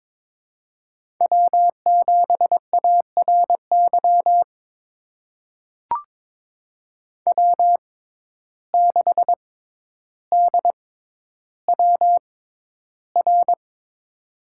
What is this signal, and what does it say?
Morse code, heard as a keyed tone, about 22 wpm, spaced as for 11 wpm, 700 Hz. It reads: W7ARY W6DWR
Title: USA Call Signs - 5 Characters - 6x Character Spacing - 22wpm